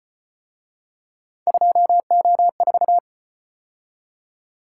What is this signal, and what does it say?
Morse code, heard as a keyed tone, about 34 wpm, 700 Hz. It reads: 2O4